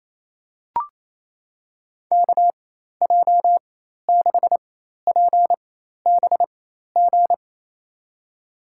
Morse code, heard as a keyed tone, about 28 wpm, spaced as for 13 wpm, 700 Hz. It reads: KJ6PBG
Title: USA Call Signs - 6 Characters - 4x Character Spacing - 28wpm